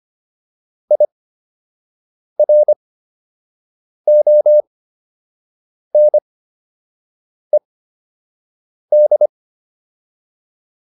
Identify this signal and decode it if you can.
Morse code, heard as a keyed tone, about 25 wpm, spaced as for 6 wpm, 600 Hz. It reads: IRONED